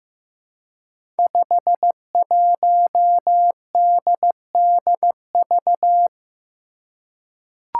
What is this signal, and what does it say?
Morse code, heard as a keyed tone, about 15 wpm, 700 Hz. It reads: 51DDV
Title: Five Challenging Letter-Number Combinations 15wpm